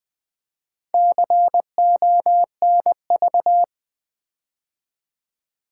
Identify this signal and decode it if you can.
Morse code, heard as a keyed tone, about 20 wpm, 700 Hz. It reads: CONV
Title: Build A Word - 4 Letters At A Time - 12-Letter Words - 20wpm